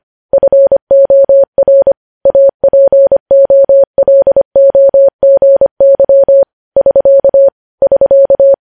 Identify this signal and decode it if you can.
Morse code, heard as a keyed tone, about 25 wpm, 550 Hz. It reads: FOR APOLOGY <SK> <SK>